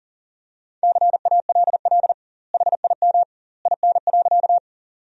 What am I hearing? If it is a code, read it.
Morse code, heard as a keyed tone, about 40 wpm, 700 Hz. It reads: CALL HIM IN.
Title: 2x Word-Spacing - Sentences from Top 100 Words - 40wpm